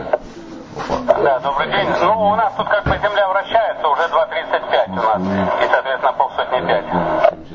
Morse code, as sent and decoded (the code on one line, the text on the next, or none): none